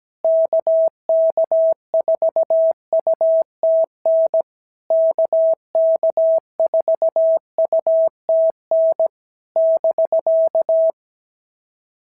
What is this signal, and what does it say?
Morse code, heard as a keyed tone, about 17 wpm, 660 Hz. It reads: KK4UTN KK4UTN <BK>
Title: SOTA QSO Practice 17wpm